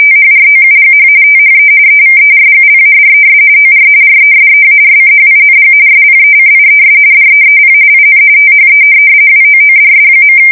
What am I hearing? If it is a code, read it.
RTTY, about 45.45 baud. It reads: THROUGH THE EVERGLADES OR RESEARCHING SPECIES DEEP IN THE WOODS.
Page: 170 Shift 2125/2295